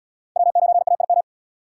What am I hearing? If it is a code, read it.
Morse code, heard as a keyed tone, about 65 wpm, 700 Hz. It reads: W9AIM